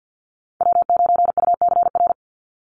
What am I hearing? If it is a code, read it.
Morse code, heard as a keyed tone, about 50 wpm, 700 Hz. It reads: W9UCR
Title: Call Signs - 50wpm